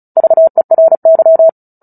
Morse code, heard as a keyed tone, about 34 wpm, 650 Hz. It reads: VERY
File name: Book Two / 09-Chapter 9 - 35 WPM